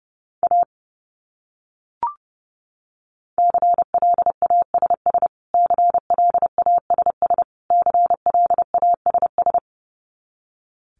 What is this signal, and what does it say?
Morse code, heard as a keyed tone, about 28 wpm, 700 Hz. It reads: A CLASS CLASS CLASS
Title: Top 500 Words Repeated 30wpm